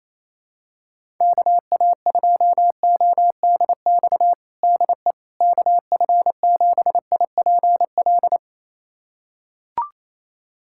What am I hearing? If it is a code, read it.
Morse code, heard as a keyed tone, about 28 wpm, 700 Hz. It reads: KA2ODX DE KF7IPL
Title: USA Callsign DE Callsign - 6 Characters - 28wpm